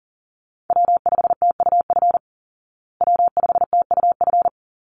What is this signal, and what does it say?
Morse code, heard as a keyed tone, about 40 wpm, 700 Hz. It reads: W5TUF W5TUF
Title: Call Signs Repeated 3x - 40wpm